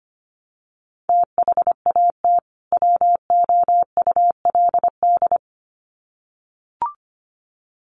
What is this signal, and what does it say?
Morse code, heard as a keyed tone, about 25 wpm, 700 Hz. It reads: THAT WOULD
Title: Sets of 2 Words 25wpm